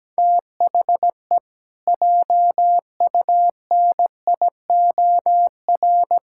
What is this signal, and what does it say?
Morse code, heard as a keyed tone, about 17 wpm, 700 Hz. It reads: THE JUNIOR
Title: Pangram Challenge v2 - 17wpm